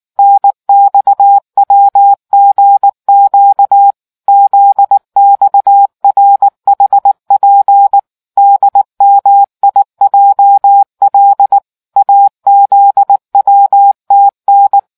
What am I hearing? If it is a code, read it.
Morse code, heard as a keyed tone, about 19 wpm, 800 Hz. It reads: NXWGQ ZXRHP DMIJL AZWTN